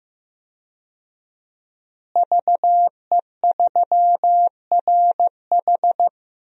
Morse code, heard as a keyed tone, about 15 wpm, 700 Hz. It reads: VE3RH
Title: DXCC-100 Call Signs - 5 Characters - Repeated 2x - 15wpm